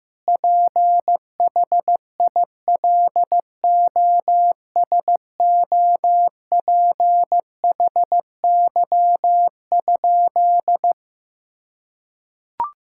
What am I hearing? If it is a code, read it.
Morse code, heard as a keyed tone, about 15 wpm, 700 Hz. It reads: PHILOSOPHY?
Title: Sentences from Top 1000 Words - Review Part 6 - 15wpm